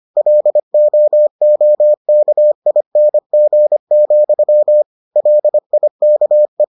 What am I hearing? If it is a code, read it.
Morse code, heard as a keyed tone, about 25 wpm, 600 Hz. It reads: LOOKING, LIKE